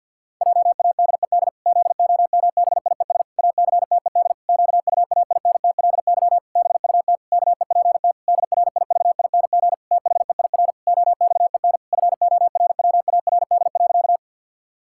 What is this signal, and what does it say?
Morse code, heard as a keyed tone, about 50 wpm, 700 Hz. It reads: JADED ZOMBIES ACTED QUAINTLY BUT KEPT DRIVING THEIR OXEN FORWARD.
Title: Pangram Challenge v2 - 50wpm